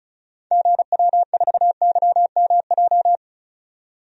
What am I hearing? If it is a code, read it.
Morse code, heard as a keyed tone, about 34 wpm, 700 Hz. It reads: GW4YMJ
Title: DXCC-100 Call Signs - 6 Characters - 35wpm